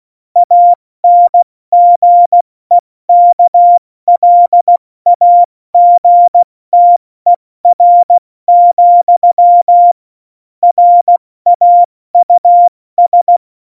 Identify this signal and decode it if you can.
Morse code, heard as a keyed tone, about 16 wpm, 700 Hz. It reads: ANGEKLAGTER, RAUS
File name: Morsekurs / Witze / Witz 05. Januar 2021, 70 F80